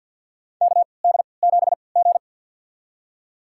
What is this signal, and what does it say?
Morse code, heard as a keyed tone, about 51 wpm, spaced as for 28 wpm, 700 Hz. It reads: KD7G
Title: USA Call Signs - 4 Characters - 3x Character Spacing - 50wpm